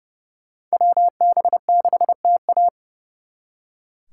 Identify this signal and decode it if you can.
Morse code, heard as a keyed tone, about 38 wpm, 700 Hz. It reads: WB6TA